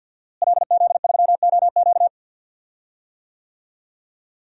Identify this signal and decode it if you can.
Morse code, heard as a keyed tone, about 47 wpm, 700 Hz. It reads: LZ3OQ
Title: DXCC-100 Call Signs - 5 Characters - Repeated 3x - 50wpm